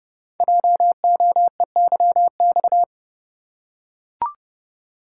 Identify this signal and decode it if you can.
Morse code, heard as a keyed tone, about 30 wpm, 700 Hz. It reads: JOEYX
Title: Code Groups - 5 Letter:Numbers - 30wpm